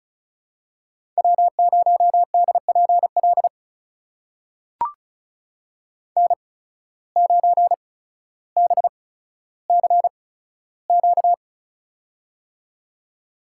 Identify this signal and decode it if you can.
Morse code, heard as a keyed tone, about 36 wpm, spaced as for 15 wpm, 700 Hz. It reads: W0DPL N9BCQ
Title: USA Call Signs - 8x Character Spacing - 35wpm